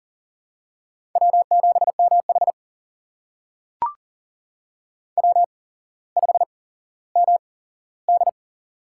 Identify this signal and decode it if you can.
Morse code, heard as a keyed tone, about 43 wpm, spaced as for 17 wpm, 700 Hz. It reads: W7MH W5MD
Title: USA Call Signs - 4 Characters - 8x Character Spacing - 40wpm